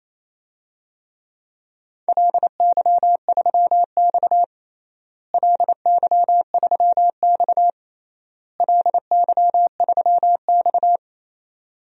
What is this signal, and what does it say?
Morse code, heard as a keyed tone, about 28 wpm, 700 Hz. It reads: LY3X LY3X LY3X
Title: DXCC-100 Call Signs - 4 Characters - Repeated 3x - 28wpm